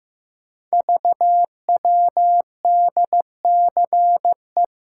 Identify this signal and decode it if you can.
Morse code, heard as a keyed tone, about 15 wpm, 700 Hz. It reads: VWDCE
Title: Code Groups - 5 Letters - 15wpm